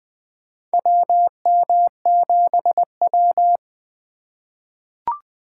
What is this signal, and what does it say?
Morse code, heard as a keyed tone, about 20 wpm, 700 Hz. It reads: WM7W
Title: USA Call Signs - 3x Character Spacing - 20wpm